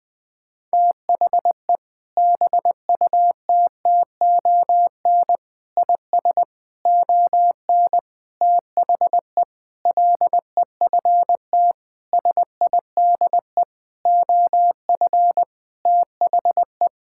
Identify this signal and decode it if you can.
Morse code, heard as a keyed tone, about 20 wpm, 700 Hz. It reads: THE BUTTON IS ON THE LEFT SIDE OF THE
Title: Sentences with 6-Word Phrases - Spoken Beginning - 20wpm